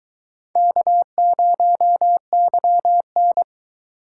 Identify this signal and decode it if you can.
Morse code, heard as a keyed tone, about 24 wpm, 700 Hz. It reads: K0YN